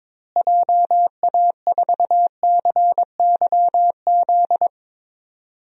Morse code, heard as a keyed tone, about 22 wpm, 700 Hz. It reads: JA4CYZ
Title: DXCC-100 Call Signs - 6 Characters - Repeated 3x - 22wpm